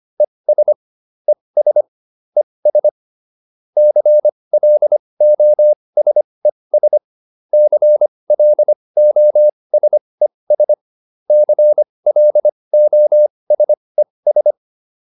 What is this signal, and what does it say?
Morse code, heard as a keyed tone, about 25 wpm, spaced as for 20 wpm, 600 Hz. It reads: ES ES ES CLOSES CLOSES CLOSES